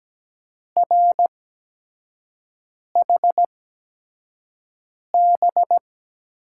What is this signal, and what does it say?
Morse code, heard as a keyed tone, about 17 wpm, spaced as for 5 wpm, 700 Hz. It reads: RHB
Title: USA Call Signs - 6 Characters - 8x Character Spacing - 17wpm